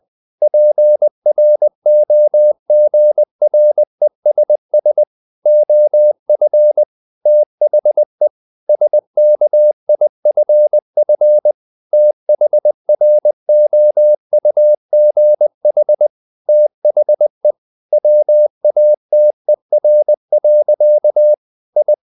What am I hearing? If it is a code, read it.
Morse code, heard as a keyed tone, about 20 wpm, 600 Hz. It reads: PROGRESS OF THE SKIFF THROUGH THE WATER. I
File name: the old man and the sea / the old man and the sea part 1 20wpm